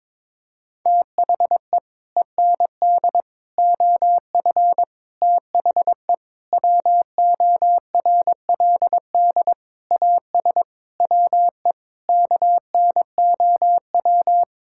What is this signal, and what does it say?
Morse code, heard as a keyed tone, about 22 wpm, 700 Hz. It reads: THE END OF THE WORLD AS WE KNOW
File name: Sets of 9 Words - 22wpm